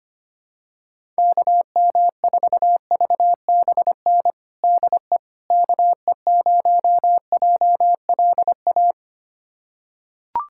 Morse code, heard as a keyed tone, about 25 wpm, 700 Hz. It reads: KM4VBN DE KE0JLA
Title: USA Callsign DE Callsign - 6 Characters - 25wpm